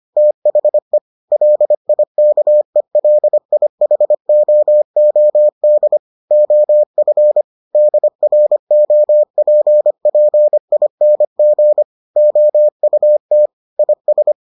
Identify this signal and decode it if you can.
Morse code, heard as a keyed tone, about 25 wpm, 600 Hz. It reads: THE LIKELIHOOD OF DROPPING OUT IS